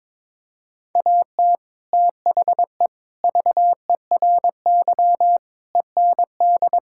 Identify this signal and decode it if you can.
Morse code, heard as a keyed tone, about 22 wpm, 700 Hz. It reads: AT THE VERY END
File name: practice / 22wpm / Sets of 6 Words - 22wpm